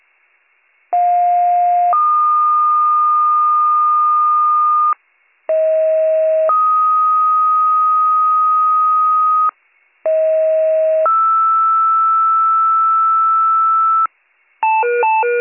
Two-tone page, heard as a one-second tone, then a three-second tone, about 710 then 1220 Hz, about 640 then 1250 Hz, about 640 then 1320 Hz. It is heard from 0.9 to 14.1 s.